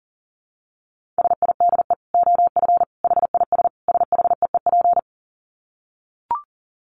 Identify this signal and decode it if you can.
Morse code, heard as a keyed tone, about 40 wpm, 700 Hz. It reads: SIDE OF HIS SHEEP
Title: Dit Challenge 40wpm